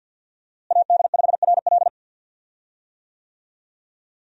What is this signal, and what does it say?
Morse code, heard as a keyed tone, about 50 wpm, 700 Hz. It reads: AD5RL
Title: USA Call Signs - 5 Characters - Repeated 3x - 50wpm